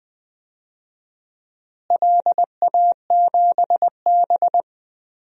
Morse code, heard as a keyed tone, about 20 wpm, 700 Hz. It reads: LA7B
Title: DXCC-100 Call Signs - 4 Characters - Repeated 2x - 20wpm